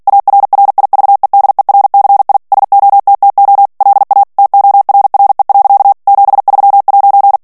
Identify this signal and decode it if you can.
Morse code, heard as a keyed tone, about 48 wpm, 800 Hz. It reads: ARRIVEDERCI SOTTO LA TORRE. 73'
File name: morse mp3 / ITALIEN texte clair 1 vitesse 45 mots minute